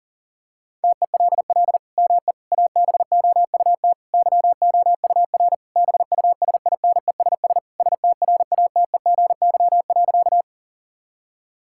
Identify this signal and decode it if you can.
Morse code, heard as a keyed tone, about 40 wpm, 700 Hz. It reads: TELL ME ABOUT YOUR BUSINESS STRATEGY.
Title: Sentences from Top 1000 Words - Review Part 5 - 40wpm